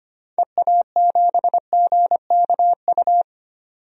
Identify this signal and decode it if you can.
Morse code, heard as a keyed tone, about 25 wpm, 700 Hz. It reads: EA7GKU
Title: DXCC-100 Call Signs - 6 Characters - 25wpm